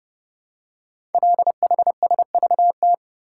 Morse code, heard as a keyed tone, about 30 wpm, 700 Hz. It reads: LHSVT
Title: Code Groups - 5 Letters - 30wpm